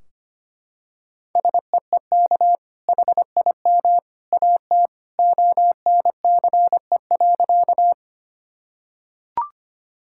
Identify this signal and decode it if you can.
Morse code, heard as a keyed tone, about 25 wpm, 700 Hz. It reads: SEEK HIM AT ONCE.
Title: Sentences from Top 1600 Words 25wpm